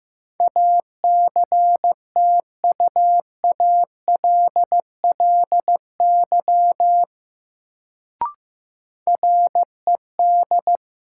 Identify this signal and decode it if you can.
Morse code, heard as a keyed tone, about 15 wpm, 700 Hz. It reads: ACTUALLY RED
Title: Sentences from Top 700 Words - New Word Drill - 15wpm